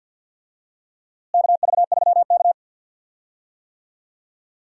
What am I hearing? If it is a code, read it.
Morse code, heard as a keyed tone, about 51 wpm, 700 Hz. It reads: KV3K
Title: USA Call Signs - 4 Characters - Repeated 2x - 50wpm